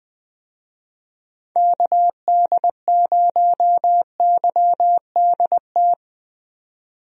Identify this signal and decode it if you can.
Morse code, heard as a keyed tone, about 20 wpm, 700 Hz. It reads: KD0YDT
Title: USA Call Signs - 6 Characters - 20wpm